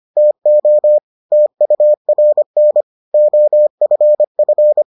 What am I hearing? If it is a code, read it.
Morse code, heard as a keyed tone, about 25 wpm, 600 Hz. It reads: TO TURN OFF